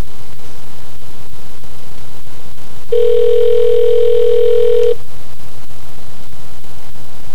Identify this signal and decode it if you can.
Ringback tone (North America), 440 + 480 Hz.